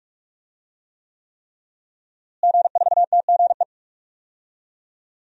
Morse code, heard as a keyed tone, about 46 wpm, 700 Hz. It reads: G4TGE
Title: DXCC-100 Call Signs - 5 Characters - Repeated 3x - 45wpm